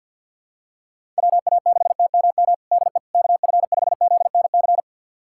Morse code, heard as a keyed tone, about 51 wpm, 700 Hz. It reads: WA6TMM DE KF5ZNC
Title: USA Callsign DE Callsign - 6 Characters - 50wpm